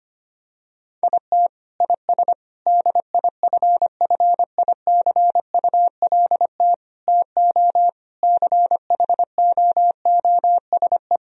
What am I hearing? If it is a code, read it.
Morse code, heard as a keyed tone, about 25 wpm, 700 Hz. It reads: IT IS DIFFICULT TO CHOOSE
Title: Sentences from Top 1000 Words - Review Part 3 - 25wpm